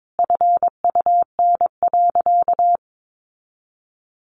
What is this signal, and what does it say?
Morse code, heard as a keyed tone, about 22 wpm, 700 Hz. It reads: FUN.
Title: Easy Phrases 22wpm